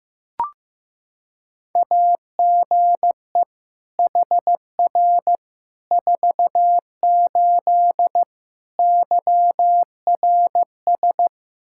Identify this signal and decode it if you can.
Morse code, heard as a keyed tone, about 16 wpm, 700 Hz. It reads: AGE HR 48 YRS
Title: Practice QSO - Exchange 3 - More Info - 15wpm